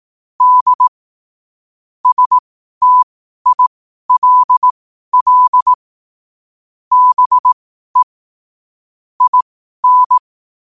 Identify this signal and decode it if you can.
Morse code, heard as a keyed tone, about 18 wpm, spaced as for 12 wpm, 1000 Hz. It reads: D STILL BE IN
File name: qotdfiles / 12wpm